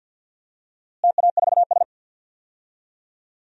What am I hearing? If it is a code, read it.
Morse code, heard as a keyed tone, about 52 wpm, 700 Hz. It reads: TA4S
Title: DXCC-100 Call Signs - 4 Characters - Repeated 3x - 50wpm